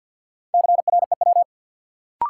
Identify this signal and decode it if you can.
Morse code, heard as a keyed tone, about 47 wpm, 700 Hz. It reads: CREW
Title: Sentences from Top 1500 Words - New Word Drill Alt - 50wpm